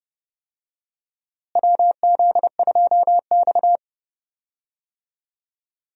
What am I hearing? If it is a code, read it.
Morse code, heard as a keyed tone, about 31 wpm, 700 Hz. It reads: WZ2X
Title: USA Call Signs - 4 Characters - Repeated 2x - 30wpm